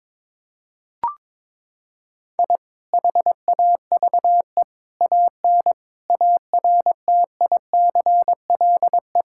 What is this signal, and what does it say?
Morse code, heard as a keyed tone, about 22 wpm, 700 Hz. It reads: I HAVE AN ARTICLE
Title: Sentences from Top 1000 Words - Review Part 5 - 22wpm